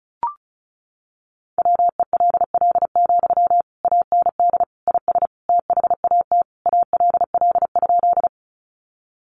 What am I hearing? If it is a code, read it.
Morse code, heard as a keyed tone, about 34 wpm, 700 Hz. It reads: WELL, AND IS THAT ALL?
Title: Sentences from Top 200 Words 35wpm